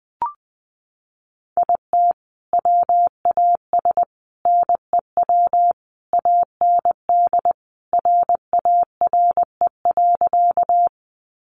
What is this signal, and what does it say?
Morse code, heard as a keyed tone, about 20 wpm, 700 Hz. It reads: IT WAS NEW AND RARE.
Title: Sentences from Top 1100 Words 20wpm